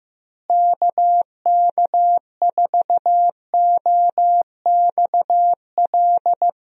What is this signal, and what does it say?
Morse code, heard as a keyed tone, about 15 wpm, 700 Hz. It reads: KK4OXL
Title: USA Call Signs - 6 Characters - Repeated 2x - 15wpm